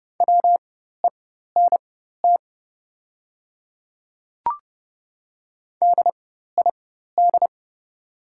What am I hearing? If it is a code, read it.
Morse code, heard as a keyed tone, about 30 wpm, spaced as for 14 wpm, 700 Hz. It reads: WENT DID
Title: 4x Character Spacing - Top 300 Words - 30wpm